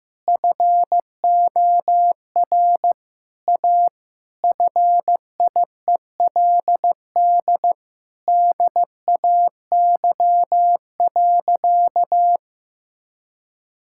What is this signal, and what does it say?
Morse code, heard as a keyed tone, about 15 wpm, 700 Hz. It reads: FOR A FIELD DAY.
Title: Sentences from Top 1000 Words - Review Part 6 - 15wpm